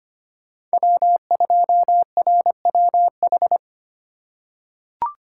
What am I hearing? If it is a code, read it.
Morse code, heard as a keyed tone, about 25 wpm, 700 Hz. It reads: W2RWH
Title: USA Call Signs - 5 Characters - 4x Character Spacing - 25wpm